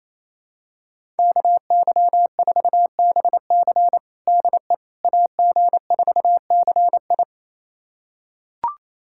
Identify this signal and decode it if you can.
Morse code, heard as a keyed tone, about 28 wpm, 700 Hz. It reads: KY4BC DE AG4CI
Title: USA Callsign DE Callsign - 5 Characters - 28wpm